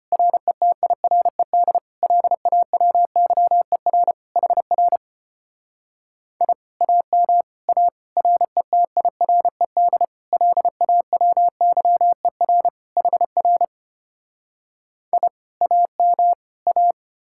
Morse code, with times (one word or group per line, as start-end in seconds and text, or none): 0.1-1.8 RETIRED
2.0-4.1 LAWYER
4.4-5.0 HR
6.4-6.5 I
6.8-7.4 AM
7.7-7.9 A
8.2-10.1 RETIRED
10.3-12.7 LAWYER
13.0-13.6 HR
15.1-15.3 I
15.6-16.3 AM
16.7-16.9 A